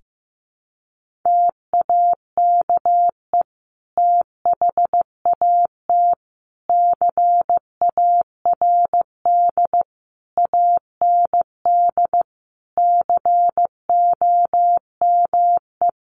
Morse code, heard as a keyed tone, about 15 wpm, 700 Hz. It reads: TAKE THAT CARD AND COME
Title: Sentences from Top 1400 Words 15wpm